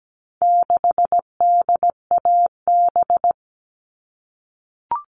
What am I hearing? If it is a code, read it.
Morse code, heard as a keyed tone, about 17 wpm, 700 Hz. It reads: 6DAB
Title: All Cut Numbers - 4-Digit 17wpm